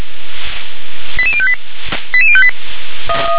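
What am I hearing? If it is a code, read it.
Five-tone selective call (ZVEI), heard as quick runs of five tones, tones of about 70 ms. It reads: 88058 88058
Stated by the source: ZVEI five-tone selective call 88104 received via SDR.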